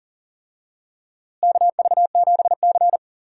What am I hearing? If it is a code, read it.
Morse code, heard as a keyed tone, about 40 wpm, 700 Hz. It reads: KV7C
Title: USA Call Signs - 4 Characters - Repeated 3x - 40wpm